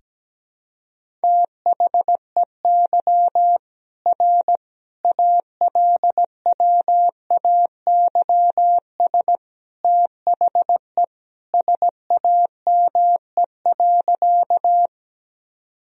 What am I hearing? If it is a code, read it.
Morse code, heard as a keyed tone, about 17 wpm, 700 Hz. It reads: THEY R ALWAYS THE SAME.
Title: Sentences from Top 300 Words and Abbreviations 17wpm